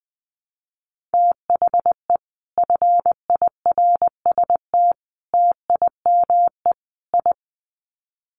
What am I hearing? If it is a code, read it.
Morse code, heard as a keyed tone, about 19 wpm, 700 Hz. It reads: THE FIRST TIME I
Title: Sets of 4 Words 20wpm